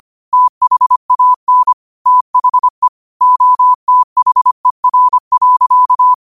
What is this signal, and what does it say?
Morse code, heard as a keyed tone, about 25 wpm, 1000 Hz. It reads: THAN THE OTHER.
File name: qotdfiles / 25wpm